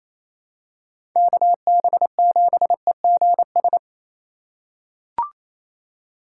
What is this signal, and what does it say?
Morse code, heard as a keyed tone, about 28 wpm, 700 Hz. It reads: KB7EGS
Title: USA Call Signs - 8x Character Spacing - 28wpm